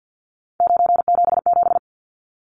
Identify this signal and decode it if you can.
Morse code, heard as a keyed tone, about 50 wpm, 700 Hz. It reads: 977
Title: Numbers - 3-Digit 50wpm